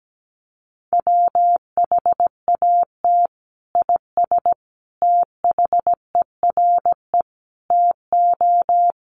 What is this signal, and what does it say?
Morse code, heard as a keyed tone, about 17 wpm, 700 Hz. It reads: WHAT IS THERE TO
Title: Sentences from Top 100 Words 17wpm